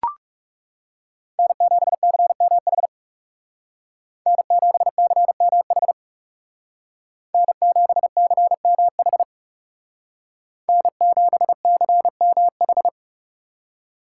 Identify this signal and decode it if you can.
Morse code, heard as a keyed tone, about 37 wpm, 700 Hz. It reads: N7CMH N7CMH N7CMH N7CMH
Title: Speed Racing - USA 1x3 Call Signs 30wpm